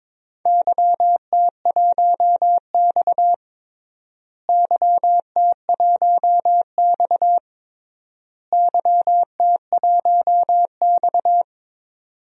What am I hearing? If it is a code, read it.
Morse code, heard as a keyed tone, about 22 wpm, 700 Hz. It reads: YT1X YT1X YT1X
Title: DXCC-100 Call Signs - 4 Characters - Repeated 3x - 22wpm